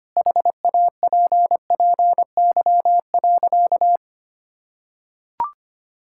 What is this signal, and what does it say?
Morse code, heard as a keyed tone, about 25 wpm, 700 Hz. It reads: HAPPY.
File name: practice / 25wpm / Sentences with 4-Word Phrases - 25wpm